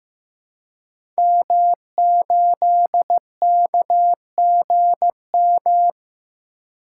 Morse code, heard as a keyed tone, about 15 wpm, 700 Hz. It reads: M8KGM
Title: Code Groups - 5 Letter:Numbers - 15wpm